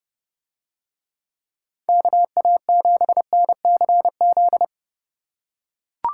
Morse code, heard as a keyed tone, about 27 wpm, 700 Hz. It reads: KA7NCZ